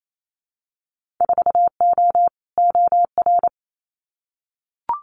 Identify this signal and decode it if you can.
Morse code, heard as a keyed tone, about 28 wpm, 700 Hz. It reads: 4O OR
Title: Winter Field Day Exchanges Encore 28wpm